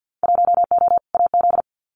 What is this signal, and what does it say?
Morse code, heard as a keyed tone, about 50 wpm, 700 Hz. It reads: 1O AZ
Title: Winter Field Day Exchanges 50wpm